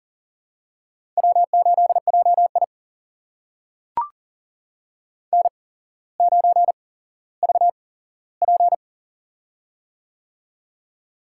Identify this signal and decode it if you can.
Morse code, heard as a keyed tone, about 40 wpm, spaced as for 17 wpm, 700 Hz. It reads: W8JI N9VP
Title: USA Call Signs - 4 Characters - 8x Character Spacing - 40wpm